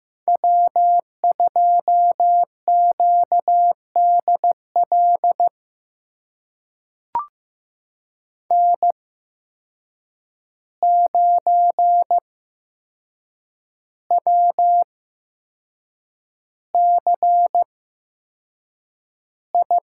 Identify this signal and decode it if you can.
Morse code, heard as a keyed tone, about 15 wpm, spaced as for 6 wpm, 700 Hz. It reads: W2QDL N9WCI
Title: USA Call Signs - 5 Characters - 8x Character Spacing - 15wpm